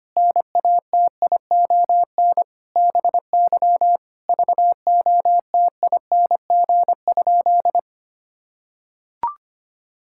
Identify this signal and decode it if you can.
Morse code, heard as a keyed tone, about 25 wpm, 700 Hz. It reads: NATION BY VOTING?
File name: practice / 25wpm / Sentences with 9-Word Phrases - 25wpm